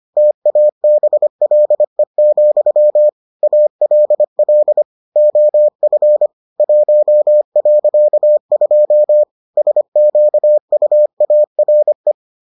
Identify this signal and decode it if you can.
Morse code, heard as a keyed tone, about 25 wpm, 600 Hz. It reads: TABLE, ALL OF 1.2 SQUARE